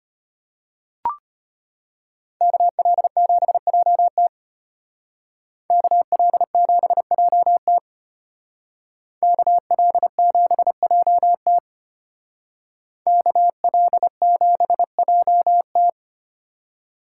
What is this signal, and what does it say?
Morse code, heard as a keyed tone, about 31 wpm, 700 Hz. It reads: KL7JT KL7JT KL7JT KL7JT
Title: Speed Racing - USA 2x2 Call Signs 25wpm